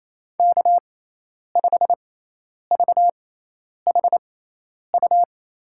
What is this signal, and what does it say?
Morse code, heard as a keyed tone, about 28 wpm, spaced as for 10 wpm, 700 Hz. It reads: K5VHU
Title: USA Call Signs - 5 Characters - 6x Character Spacing - 28wpm